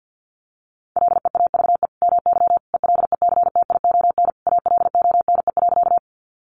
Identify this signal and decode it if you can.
Morse code, heard as a keyed tone, about 50 wpm, 700 Hz. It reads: LEAVE MY ELECTION ALONE.
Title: Sentences from Top 1500 Words 50wpm